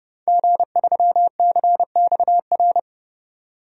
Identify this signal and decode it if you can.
Morse code, heard as a keyed tone, about 31 wpm, 700 Hz. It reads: G3CXR
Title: DXCC-100 Call Signs - 5 Characters - 30wpm